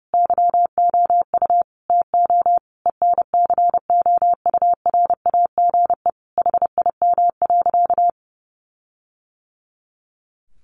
Morse code, heard as a keyed tone, about 30 wpm, 700 Hz. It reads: YOU TO ENCOURAGE HIM.
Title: Sentences from Top 1400 Words 30wpm